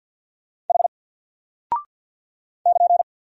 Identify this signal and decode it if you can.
Morse code, heard as a keyed tone, about 50 wpm, 700 Hz. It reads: H <KN>
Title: Single Character (Letter:Number:Punctuation:Prosign) Alt 50wpm